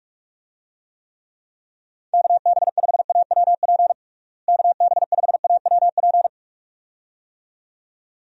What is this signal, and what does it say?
Morse code, heard as a keyed tone, about 42 wpm, 700 Hz. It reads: KB5AWP KB5AWP
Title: USA Call Signs - 6 Characters - Repeated 2x - 45wpm